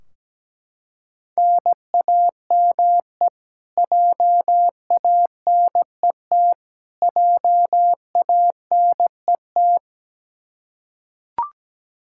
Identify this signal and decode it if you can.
Morse code, heard as a keyed tone, about 17 wpm, 700 Hz. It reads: NAME JANET JANET
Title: Practice QSO - Exchange 1 - Beginning - 17wpm